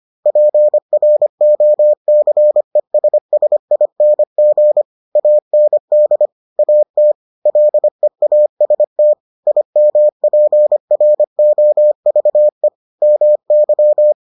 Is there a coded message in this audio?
Yes: Morse code from 0.3 to 14.1 s, heard as a keyed tone, about 25 wpm, 600 Hz. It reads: PROCESSING AND AT LEAST IMPROVE MY